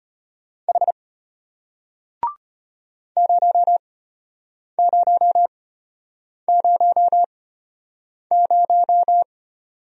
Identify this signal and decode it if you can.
Morse code, heard as a keyed tone, about 33 wpm, spaced as for 8 wpm, 700 Hz. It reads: H 0000